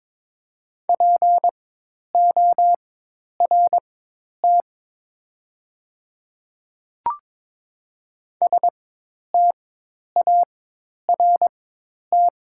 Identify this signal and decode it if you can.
Morse code, heard as a keyed tone, about 22 wpm, spaced as for 10 wpm, 700 Hz. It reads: PORT START
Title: 4x Character Spacing - Top 300 Words - 22wpm